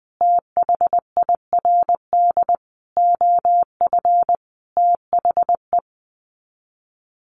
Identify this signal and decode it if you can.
Morse code, heard as a keyed tone, about 20 wpm, 700 Hz. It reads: THIRD OF THE
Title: Sets of 5 Words Encore2 20wpm